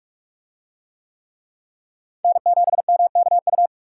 Morse code, heard as a keyed tone, about 44 wpm, 700 Hz. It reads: N7MKU